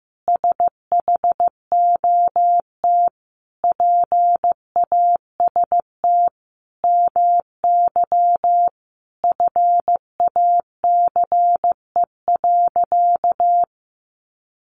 Morse code, heard as a keyed tone, about 15 wpm, 700 Hz. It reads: SHOT PAST MY FACE.
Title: Sentences from Top 1100 Words 15wpm